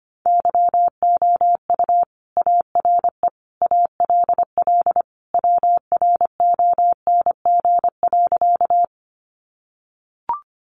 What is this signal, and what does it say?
Morse code, heard as a keyed tone, about 25 wpm, 700 Hz. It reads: YOU ARE ALL WRONG.
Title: Sentences from Top 900 Words 25wpm